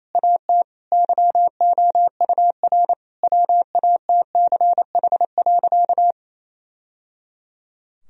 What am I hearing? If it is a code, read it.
Morse code, heard as a keyed tone, about 28 wpm, 700 Hz. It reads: AT YOUR WATCH.
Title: Sentences from Top 1000 Words - Review Part 1 - 28wpm